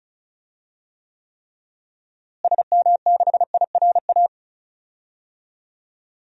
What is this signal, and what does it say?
Morse code, heard as a keyed tone, about 36 wpm, 700 Hz. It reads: SM6IRA
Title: DXCC-100 Call Signs - 6 Characters - Repeated 3x - 35wpm